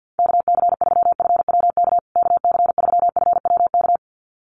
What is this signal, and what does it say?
Morse code, heard as a keyed tone, about 50 wpm, 700 Hz. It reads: KC3FWK KC3FWK
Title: USA Call Signs Repeated 2x 50wpm